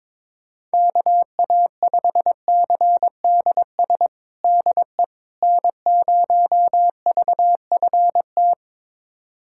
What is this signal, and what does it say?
Morse code, heard as a keyed tone, about 22 wpm, 700 Hz. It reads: KA5CDS DE N0VFT
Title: USA Callsign DE Callsign 22wpm